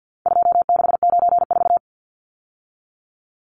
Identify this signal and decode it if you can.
Morse code, heard as a keyed tone, about 48 wpm, 700 Hz. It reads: 2694
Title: Numbers - 4-Digit 50wpm